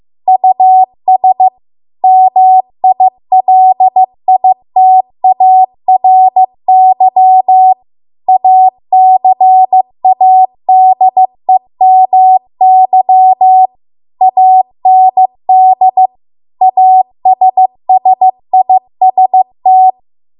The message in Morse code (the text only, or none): US MILITARY ACADEMY AND ASSIST